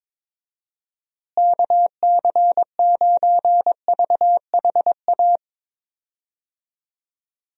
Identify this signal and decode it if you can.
Morse code, heard as a keyed tone, about 22 wpm, 700 Hz. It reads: KC9VHA
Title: USA Call Signs - 6 Characters - Repeated 3x - 22wpm